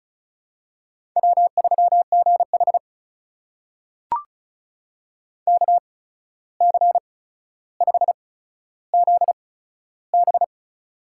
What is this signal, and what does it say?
Morse code, heard as a keyed tone, about 36 wpm, spaced as for 14 wpm, 700 Hz. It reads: W3GH KC5ZB
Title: USA Call Signs - 8x Character Spacing - 35wpm